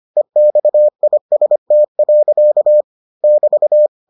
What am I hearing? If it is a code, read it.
Morse code, heard as a keyed tone, about 25 wpm, 600 Hz. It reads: EXIST. <BT>